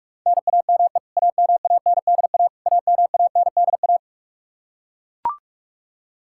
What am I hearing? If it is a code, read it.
Morse code, heard as a keyed tone, about 41 wpm, 700 Hz. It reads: NAME AMANDA AMANDA
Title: K1USN SST QSO Practice 45wpm